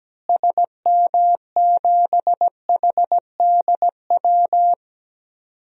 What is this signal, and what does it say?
Morse code, heard as a keyed tone, about 17 wpm, 700 Hz. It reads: SM7HDW
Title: DXCC-100 Call Signs - 6 Characters - Repeated 2x - 17wpm